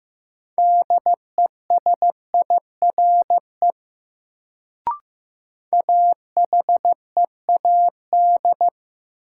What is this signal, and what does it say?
Morse code, heard as a keyed tone, about 15 wpm, 700 Hz. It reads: DESIRE AHEAD
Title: Sentences from Top 1100 Words - New Word Drill Alt - 15wpm